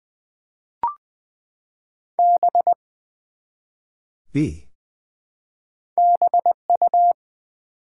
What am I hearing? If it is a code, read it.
Morse code, heard as a keyed tone, about 20 wpm, 700 Hz. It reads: B BU
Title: Build A Word - 1 Letter At A Time - 3 to 5-Letter Words - 20wpm